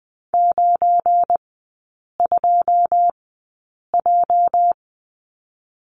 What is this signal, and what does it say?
Morse code, heard as a keyed tone, about 20 wpm, spaced as for 8 wpm, 690 Hz. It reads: 92J
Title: 205 Lesson 33e - Letter j - Three Characters 20wpm